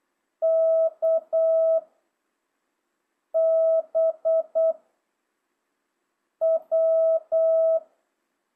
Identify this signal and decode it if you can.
Morse code, heard as a keyed tone, about 8 wpm, 660 Hz. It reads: K B W
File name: jh8cbh / 2025cw